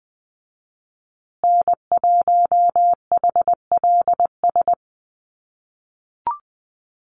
Morse code, heard as a keyed tone, about 20 wpm, 700 Hz. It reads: N1HLS
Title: USA Call Signs Repeated 3x 20wpm